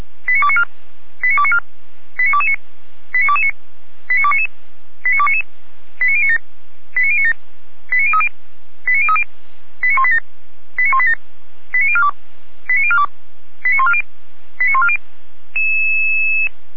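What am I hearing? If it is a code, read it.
Five-tone selective call (ZVEI), heard as quick runs of five tones, tones of about 70 ms. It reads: 89294 89294 89229 89229 89299 89299 89097 89097 89030 89030 89187 89187 89042 89042 89140 89140
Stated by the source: ZVEI five-tone call received by SDR, from the feuerwehr-orlamuende.org audio server.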